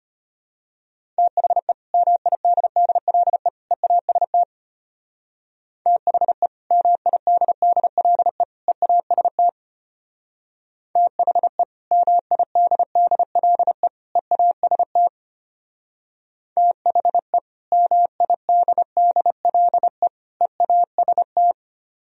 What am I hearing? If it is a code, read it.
Morse code, heard as a keyed tone, about 33 wpm, 700 Hz. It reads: THE MIDDLE EAST THE MIDDLE EAST THE MIDDLE EAST THE MIDDLE EAST